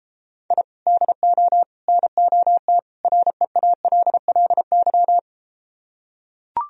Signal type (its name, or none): Morse code